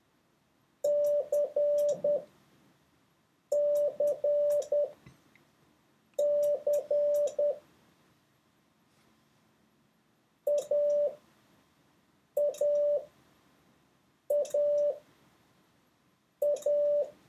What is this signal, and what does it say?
Morse code, heard as a keyed tone, about 10 wpm, spaced as for 5 wpm, 580 Hz. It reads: CCC AAAA